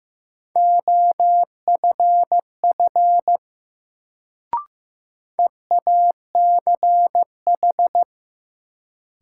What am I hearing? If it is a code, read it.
Morse code, heard as a keyed tone, about 15 wpm, 700 Hz. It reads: OFF EACH
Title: Sets of 2 Words - New Word Drill - 15wpm